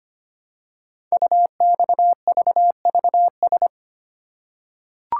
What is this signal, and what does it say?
Morse code, heard as a keyed tone, about 25 wpm, 700 Hz. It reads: UXVVS
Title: Code Groups - 5 Letters - 25wpm